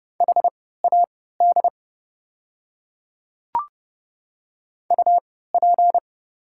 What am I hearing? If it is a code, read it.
Morse code, heard as a keyed tone, about 31 wpm, spaced as for 17 wpm, 700 Hz. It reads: HAD UP
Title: 3x Character Spacing - Top 300 Words - 30wpm